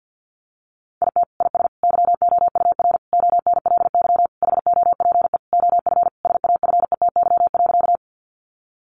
Morse code, heard as a keyed tone, about 48 wpm, 700 Hz. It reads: IT IS YOUR ONLY HOPE OF SAFETY.
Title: Sentences from Top 1000 Words 50wpm